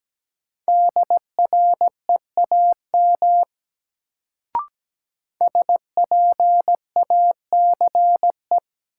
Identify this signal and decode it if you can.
Morse code, heard as a keyed tone, about 17 wpm, 700 Hz. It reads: DREAM SPACE
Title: Sentences from Top 1500 Words - New Word Drill Alt - 17wpm